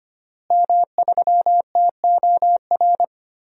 Morse code, heard as a keyed tone, about 25 wpm, 700 Hz. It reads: M3TOR
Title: DXCC-100 Call Signs - 5 Characters - Repeated 3x - 25wpm